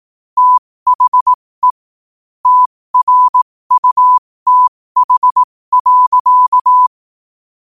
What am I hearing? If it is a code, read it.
Morse code, heard as a keyed tone, about 18 wpm, spaced as for 15 wpm, 1000 Hz. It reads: THE TRUTH.
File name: qotdfiles / 15wpm